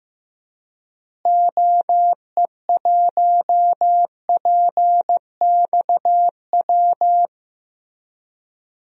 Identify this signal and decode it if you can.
Morse code, heard as a keyed tone, about 16 wpm, 700 Hz. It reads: OE1PXW
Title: DXCC-100 Call Signs - 6 Characters - Repeated 2x - 15wpm